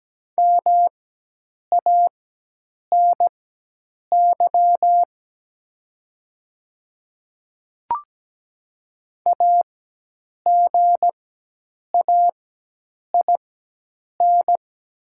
Morse code, heard as a keyed tone, about 17 wpm, spaced as for 8 wpm, 700 Hz. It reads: MANY AGAIN
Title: 4x Character Spacing - Top 200 Words - 17wpm